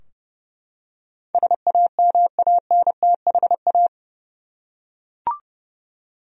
Morse code, heard as a keyed tone, about 30 wpm, 700 Hz. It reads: SAMANTHA
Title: USA Common Names - 30wpm